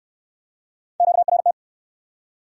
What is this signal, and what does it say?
Morse code, heard as a keyed tone, about 62 wpm, 700 Hz. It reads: QRT